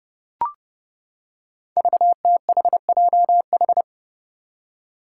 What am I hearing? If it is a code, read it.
Morse code, heard as a keyed tone, about 30 wpm, 700 Hz. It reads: VTHJH
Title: Five Challenging Letter-Number Combinations 30wpm